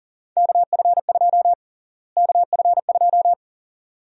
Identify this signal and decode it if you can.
Morse code, heard as a keyed tone, about 38 wpm, 700 Hz. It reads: KF2 KF2
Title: USA Call Sign Prefixes - Repeated 2x - 40wpm